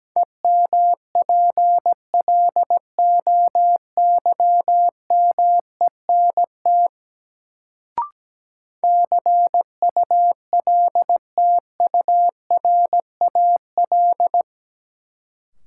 Morse code, heard as a keyed tone, about 17 wpm, 700 Hz. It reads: EMPLOYMENT CULTURAL
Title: Sentences from Top 1500 Words - New Word Drill - 17wpm